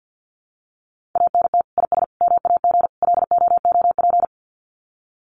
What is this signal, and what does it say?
Morse code, heard as a keyed tone, about 50 wpm, 700 Hz. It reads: ANT IS MAG LOOP
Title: Common QSO Phrases 50wpm